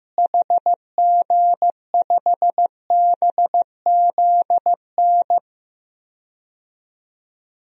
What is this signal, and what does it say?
Morse code, heard as a keyed tone, about 16 wpm, 700 Hz. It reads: HG5BZN
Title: DXCC-100 Call Signs - 6 Characters - Repeated 3x - 15wpm